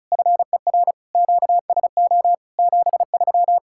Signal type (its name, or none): Morse code